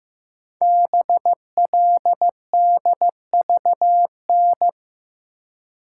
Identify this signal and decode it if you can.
Morse code, heard as a keyed tone, about 15 wpm, 700 Hz. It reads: BLDVN
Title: Code Groups - 5 Letter:Numbers - 15wpm